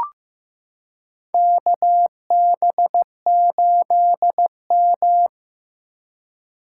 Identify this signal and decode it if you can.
Morse code, heard as a keyed tone, about 16 wpm, 700 Hz. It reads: KB8M
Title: K1USN SST QSO Practice - 15wpm